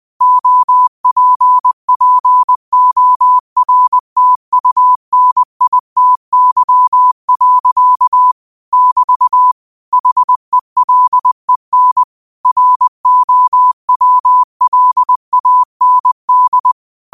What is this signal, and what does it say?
Morse code, heard as a keyed tone, about 20 wpm, 1000 Hz. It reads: OPPORTUNITY. <BT> HELEN ROWLAND
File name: qotdfiles / 20wpm